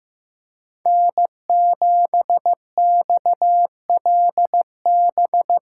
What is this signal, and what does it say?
Morse code, heard as a keyed tone, about 15 wpm, 700 Hz. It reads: N7XLB
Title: USA Call Signs - 5 Characters - 4x Character Spacing - 15wpm